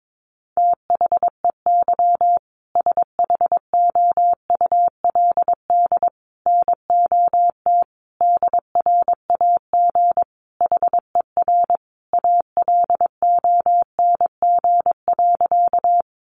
Morse code, heard as a keyed tone, about 22 wpm, 700 Hz. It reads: THEY SHOULD NOT DRAG HER ALONG.
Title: Sentences from Top 1500 Words 22wpm